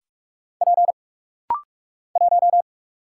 Morse code, heard as a keyed tone, about 44 wpm, 690 Hz. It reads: P 1
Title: Single Letter-Number - Rapid-Fire 45wpm